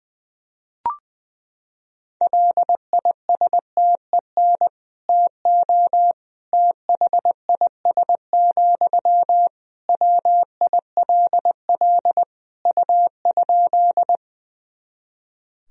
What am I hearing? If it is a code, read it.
Morse code, heard as a keyed tone, about 20 wpm, 700 Hz. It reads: LISTEN TO THIS, WILL U?
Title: Sentences from Top 400 Words and Abbreviations 20wpm